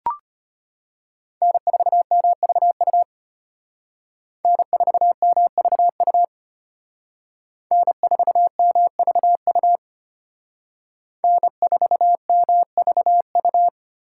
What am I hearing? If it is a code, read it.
Morse code, heard as a keyed tone, about 34 wpm, 700 Hz. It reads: N4MVU N4MVU N4MVU N4MVU